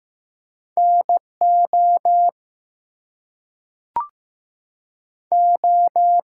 Morse code, heard as a keyed tone, about 15 wpm, 700 Hz. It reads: NO O
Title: Two Letter Words - 15wpm